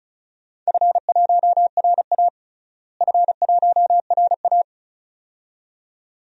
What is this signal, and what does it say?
Morse code, heard as a keyed tone, about 34 wpm, 700 Hz. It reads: F1RA F1RA
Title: DXCC-100 Call Signs - 4 Characters - Repeated 2x - 35wpm